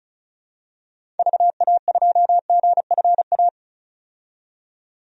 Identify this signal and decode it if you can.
Morse code, heard as a keyed tone, about 36 wpm, 700 Hz. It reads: VA2GFA
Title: DXCC-100 Call Signs - 6 Characters - 35wpm